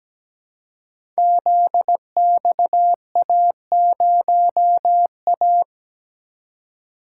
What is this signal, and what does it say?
Morse code, heard as a keyed tone, about 17 wpm, 700 Hz. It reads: ZXA0A
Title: Code Groups - 5 Letter:Numbers - 17wpm